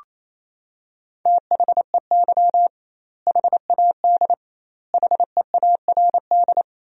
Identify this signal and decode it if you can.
Morse code, heard as a keyed tone, about 28 wpm, 700 Hz. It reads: THEY HAD HEARD
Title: 2x Word-Spacing - Sentences from Top 400 Words - 28wpm